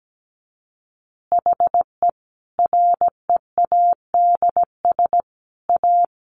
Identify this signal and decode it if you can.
Morse code, heard as a keyed tone, about 17 wpm, 700 Hz. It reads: HE READS A
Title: Additional Easy Phrases 17wpm